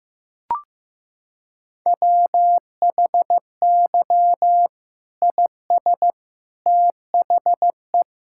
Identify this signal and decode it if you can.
Morse code, heard as a keyed tone, about 16 wpm, 700 Hz. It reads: WHY IS THE
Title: Sentences from Top 1600 Words 15wpm